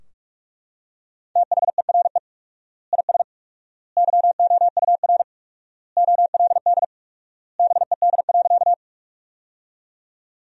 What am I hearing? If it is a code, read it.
Morse code, heard as a keyed tone, about 42 wpm, 700 Hz. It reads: THERE IS YOUR OLD BED.
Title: 4x Word-Spacing - Sentences from Top 500 Words - 45wpm